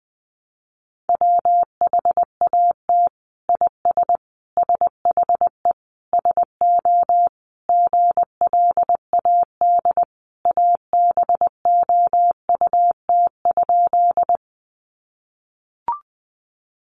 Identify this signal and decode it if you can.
Morse code, heard as a keyed tone, about 19 wpm, 700 Hz. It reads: WHAT IS SHE SO GLAD ABOUT?
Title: Sentences from Top 800 Words 20wpm